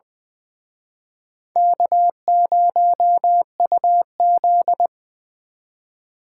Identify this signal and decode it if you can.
Morse code, heard as a keyed tone, about 24 wpm, 700 Hz. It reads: K0UZ
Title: Speed Racing - USA 1x2 Call Signs 20wpm